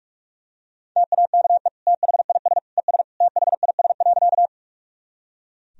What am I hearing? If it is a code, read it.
Morse code, heard as a keyed tone, about 42 wpm, 700 Hz. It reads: TAKE THIS ES THIS.
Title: Sentences from Top 200 Words and Abbreviations 45wpm